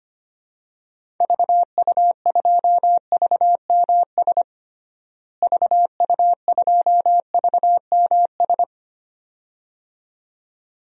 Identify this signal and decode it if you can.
Morse code, heard as a keyed tone, about 25 wpm, 700 Hz. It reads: VU2VMS VU2VMS
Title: DXCC-100 Call Signs - 6 Characters - Repeated 2x - 25wpm